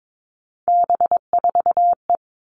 Morse code, heard as a keyed tone, about 22 wpm, 700 Hz. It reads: B4E
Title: All Cut Numbers - 3-Digit 22wpm